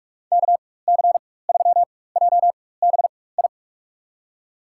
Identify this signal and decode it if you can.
Morse code, heard as a keyed tone, about 44 wpm, spaced as for 21 wpm, 700 Hz. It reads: KC3JBI
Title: USA Call Signs - 6 Characters - 4x Character Spacing - 45wpm